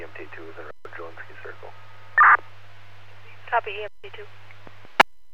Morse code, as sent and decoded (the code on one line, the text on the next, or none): none